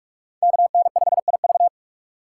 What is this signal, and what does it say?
Morse code, heard as a keyed tone, about 44 wpm, 700 Hz. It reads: KN5IV
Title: Code Groups - 5 Letter:Numbers - 45wpm